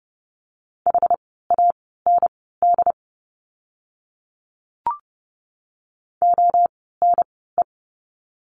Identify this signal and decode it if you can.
Morse code, heard as a keyed tone, about 30 wpm, spaced as for 17 wpm, 700 Hz. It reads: HAND ONE
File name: course / 30:10wpm / 230 Lesson 37f - Letter x - Words 30wpm